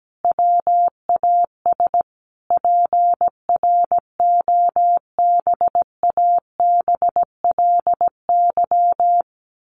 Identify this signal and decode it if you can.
Morse code, heard as a keyed tone, about 17 wpm, 700 Hz. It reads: WAS PROBABLY
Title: Sentences from Top 1400 Words 17wpm